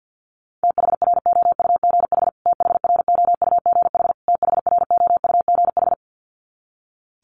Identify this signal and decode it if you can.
Morse code, heard as a keyed tone, about 50 wpm, 700 Hz. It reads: THROUGH THROUGH THROUGH
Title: Top 500 Words Repeated 50wpm